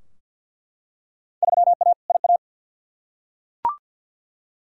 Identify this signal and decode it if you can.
Morse code, heard as a keyed tone, about 49 wpm, 700 Hz. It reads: 3A IA
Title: Field Day Exchanges Encore - 50wpm